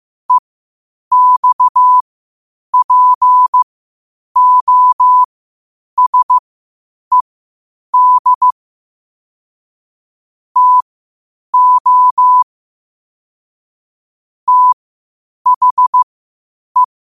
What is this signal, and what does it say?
Morse code, heard as a keyed tone, about 15 wpm, spaced as for 8 wpm, 1000 Hz. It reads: EXPOSED TO THE
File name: qotdfiles / 8wpm